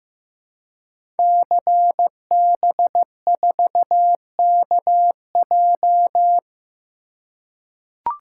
Morse code, heard as a keyed tone, about 15 wpm, 700 Hz. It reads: CB4KJ
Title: Code Groups - 5 Letter:Numbers - 15wpm